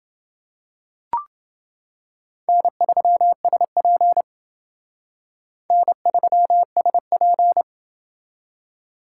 Morse code, heard as a keyed tone, about 28 wpm, 700 Hz. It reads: N3SP N3SP